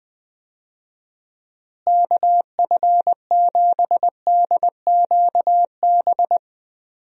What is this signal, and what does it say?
Morse code, heard as a keyed tone, about 20 wpm, 700 Hz. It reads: KF7DQB
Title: USA Call Signs - 6 Characters - Repeated 2x - 20wpm